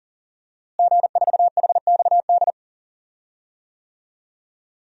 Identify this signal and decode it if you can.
Morse code, heard as a keyed tone, about 37 wpm, 700 Hz. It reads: G4HXD
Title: DXCC-100 Call Signs - 5 Characters - Repeated 2x - 40wpm